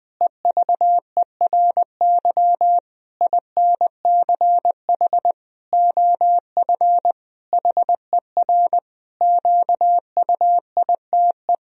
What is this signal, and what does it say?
Morse code, heard as a keyed tone, about 20 wpm, 700 Hz. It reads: EVERY INCH OF HER QUITE
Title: Sentences from Top 1000 Words - Review Part 3 - 20wpm